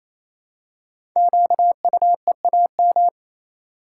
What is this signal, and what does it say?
Morse code, heard as a keyed tone, about 27 wpm, 700 Hz. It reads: QUEAM
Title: Code Groups - 5 Letters - 28wpm